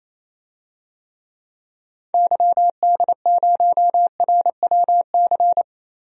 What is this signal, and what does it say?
Morse code, heard as a keyed tone, about 28 wpm, 700 Hz. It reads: YD0RWC
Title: DXCC-100 Call Signs - 6 Characters - Repeated 2x - 28wpm